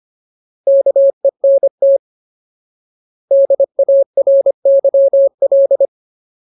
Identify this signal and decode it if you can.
Morse code, heard as a keyed tone, about 25 wpm, 550 Hz. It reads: KENT DARYL
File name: Ultimate Word List / 25WPM Ultimate